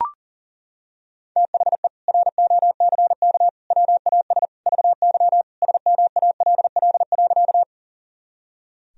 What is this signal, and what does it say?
Morse code, heard as a keyed tone, about 40 wpm, 700 Hz. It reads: THE ROCK WAS VY SMALL.
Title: Sentences from Top 400 Words and Abbreviations 40wpm